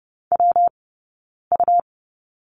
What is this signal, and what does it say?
Morse code, heard as a keyed tone, about 30 wpm, 700 Hz. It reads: W U
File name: Lesson 19d - Letter w - Two Characters 30wpm